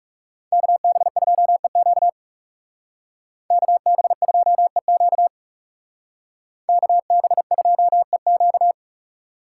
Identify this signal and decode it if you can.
Morse code, heard as a keyed tone, about 38 wpm, 700 Hz. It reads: KB2EQ KB2EQ KB2EQ